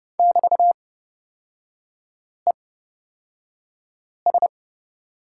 Morse code, heard as a keyed tone, about 30 wpm, spaced as for 5 wpm, 700 Hz. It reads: <BT>ES
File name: Lesson 36e - BT - Three Characters 30wpm